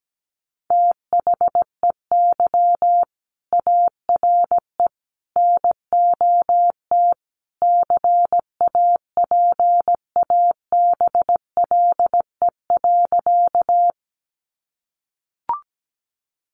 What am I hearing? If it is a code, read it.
Morse code, heard as a keyed tone, about 17 wpm, 700 Hz. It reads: THEY ARE NOT CAPABLE.
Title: Sentences from Top 1200 Words 17wpm